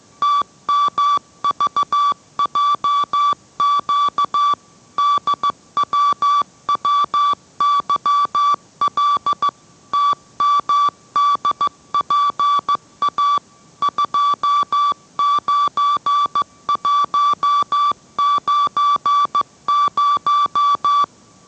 Morse code, heard as a keyed tone, about 15 wpm, 1160 Hz. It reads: TMVJQ DWWYL TMDPA 29190